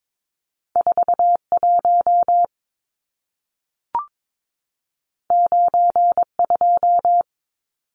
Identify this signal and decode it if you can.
Morse code, heard as a keyed tone, about 22 wpm, 700 Hz. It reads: 41 92
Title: Numbers - 2-Digit 22wpm